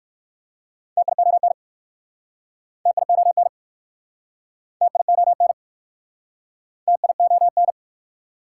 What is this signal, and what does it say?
Morse code, heard as a keyed tone, about 62 wpm, 700 Hz. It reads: TION TION TION TION